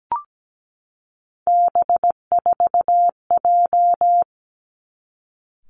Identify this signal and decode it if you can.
Morse code, heard as a keyed tone, about 17 wpm, 700 Hz. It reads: B4J
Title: Three Challenging Letter-Number Combinations 17wpm